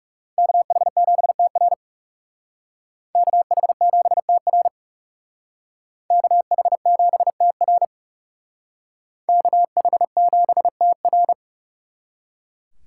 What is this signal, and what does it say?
Morse code, heard as a keyed tone, about 37 wpm, 700 Hz. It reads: KH7TR KH7TR KH7TR KH7TR
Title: Speed Racing - USA 2x2 Call Signs 30wpm